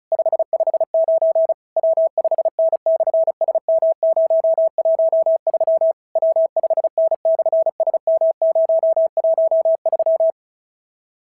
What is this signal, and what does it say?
Morse code, heard as a keyed tone, about 33 wpm, 650 Hz. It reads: 559 W5N/SM013 W5N/SM013
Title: SOTA QSO Practice 35wpm